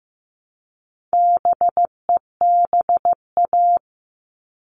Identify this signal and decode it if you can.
Morse code, heard as a keyed tone, about 15 wpm, 700 Hz. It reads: BEBA
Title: All Cut Numbers - 4-Digit 15wpm